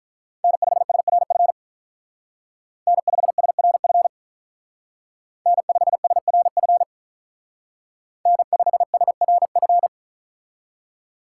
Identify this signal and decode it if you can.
Morse code, heard as a keyed tone, about 43 wpm, 700 Hz. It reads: N5SRF N5SRF N5SRF N5SRF
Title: Speed Racing - USA Call Signs 35wpm